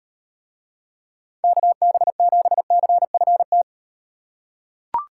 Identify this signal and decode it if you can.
Morse code, heard as a keyed tone, about 38 wpm, 700 Hz. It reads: KB7CFT